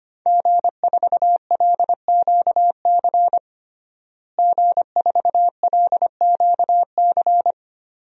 Morse code, heard as a keyed tone, about 25 wpm, 700 Hz. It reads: G4LQC G4LQC
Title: DXCC-100 Call Signs - 5 Characters - Repeated 2x - 25wpm